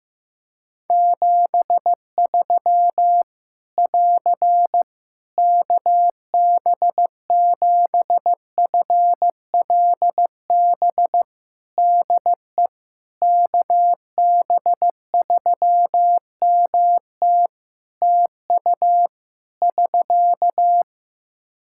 Morse code, heard as a keyed tone, about 15 wpm, 700 Hz. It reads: 73 <AR> KB7FLB DE KB3MT TU <SK>
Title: Practice QSO - Exchange 4 - Ending - 15wpm